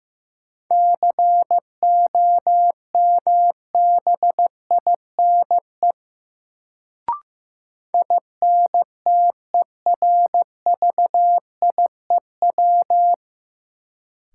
Morse code, heard as a keyed tone, about 15 wpm, 700 Hz. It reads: COMBINE INTERVIEW
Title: Sentences from Top 1600 Words - New Word Drill Alt - 15wpm